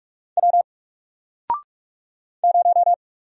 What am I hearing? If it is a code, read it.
Morse code, heard as a keyed tone, about 45 wpm, 700 Hz. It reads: W 0
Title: Single Letter-Number Alt 45wpm